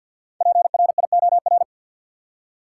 Morse code, heard as a keyed tone, about 50 wpm, 700 Hz. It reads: PRIOR
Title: Five Letter Words - 50wpm